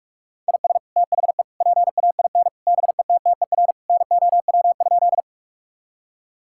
Morse code, heard as a keyed tone, about 46 wpm, 700 Hz. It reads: IS THE PAIN BETTER NOW?
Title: Sentences from Top 1600 Words 45wpm